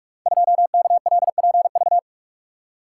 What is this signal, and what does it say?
Morse code, heard as a keyed tone, about 44 wpm, 700 Hz. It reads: 2KLPV
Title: Code Groups - 5 Letter:Numbers - 45wpm